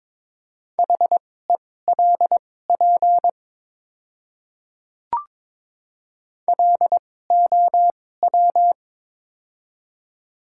Morse code, heard as a keyed tone, about 23 wpm, spaced as for 16 wpm, 700 Hz. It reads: HELP LOW
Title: 2x Character Spacing - Top 300 Words - 22wpm